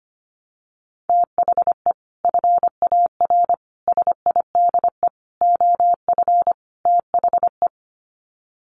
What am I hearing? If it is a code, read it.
Morse code, heard as a keyed tone, about 25 wpm, 700 Hz. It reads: THE FAR SIDE OF THE
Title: Sets of 5 Words 25wpm